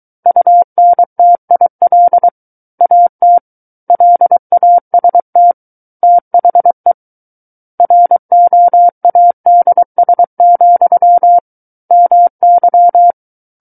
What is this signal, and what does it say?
Morse code, heard as a keyed tone, about 23 wpm, 700 Hz. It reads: UNTIL AT LAST THE ROADS, MY
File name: Book One / 14-Chapter 14 - 23 WPM